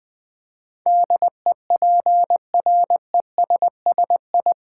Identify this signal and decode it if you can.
Morse code, heard as a keyed tone, about 20 wpm, 700 Hz. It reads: DEPRESSI
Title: Build A Word - 4 Letters At A Time - 10-Letter Words - 20wpm